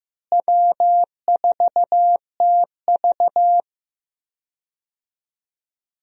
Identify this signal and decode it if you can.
Morse code, heard as a keyed tone, about 15 wpm, 700 Hz. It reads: W4TV
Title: USA Call Signs - 4 Characters - Repeated 2x - 15wpm